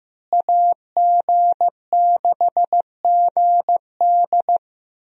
Morse code, heard as a keyed tone, about 15 wpm, 700 Hz. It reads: AG6GD
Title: USA Call Signs - 5 Characters - Repeated 2x - 15wpm